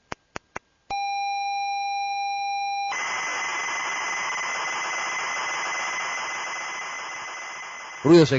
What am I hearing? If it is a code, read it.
ZX Spectrum tape loading.